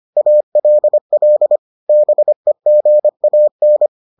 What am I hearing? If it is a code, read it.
Morse code, heard as a keyed tone, about 25 wpm, 600 Hz. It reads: ALL BEGAN